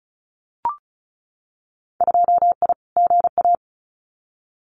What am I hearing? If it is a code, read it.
Morse code, heard as a keyed tone, about 36 wpm, 700 Hz. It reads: 2I GA
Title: Winter Field Day Exchanges 35wpm